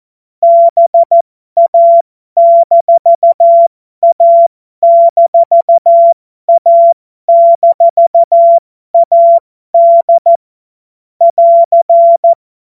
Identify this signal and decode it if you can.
Morse code, heard as a keyed tone, about 14 wpm, spaced as for 12 wpm, 680 Hz. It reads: BA-A-A-AD <AR>
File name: Morsekurs / Witze